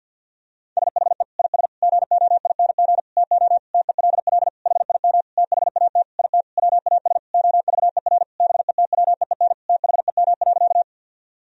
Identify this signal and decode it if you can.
Morse code, heard as a keyed tone, about 52 wpm, 700 Hz. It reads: SHE IS GOING TO TELL HIM THAT IT WAS OVER BETWEEN THEM.
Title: Sentences from Top 1000 Words - Review Part 1 - 50wpm